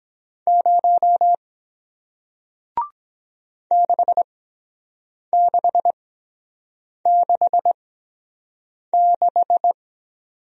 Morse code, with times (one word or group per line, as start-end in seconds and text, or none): 0.5-1.3 0
3.7-9.7 6666